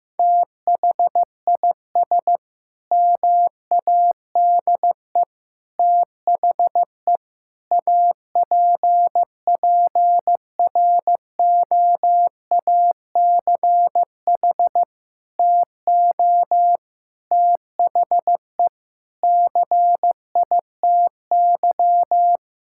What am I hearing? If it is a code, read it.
Morse code, heard as a keyed tone, about 15 wpm, 700 Hz. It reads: THIS MADE THE APPROACH TO THE CITY
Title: Sentences from Top 1000 Words - Review Part 6 - 15wpm